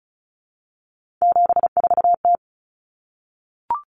Morse code, heard as a keyed tone, about 34 wpm, 700 Hz. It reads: 74T
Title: Common Cut Numbers - 3-Digit 35wpm